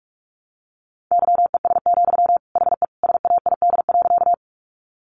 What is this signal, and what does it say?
Morse code, heard as a keyed tone, about 44 wpm, 700 Hz. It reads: YES, HE SAID.
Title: Sentences from Top 500 Words 45wpm